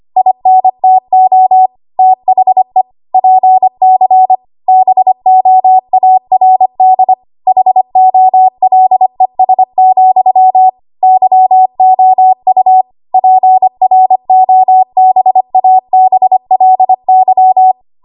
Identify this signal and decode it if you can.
Morse code, heard as a keyed tone, about 25 wpm, 750 Hz. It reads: INTO THE PC BOARD HOLES, YOU PROBABLY
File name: Archive / 25 WPM / 25WPM